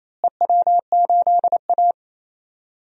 Morse code, heard as a keyed tone, about 28 wpm, 700 Hz. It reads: EW8A
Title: DXCC-100 Call Signs - 4 Characters - Repeated 2x - 28wpm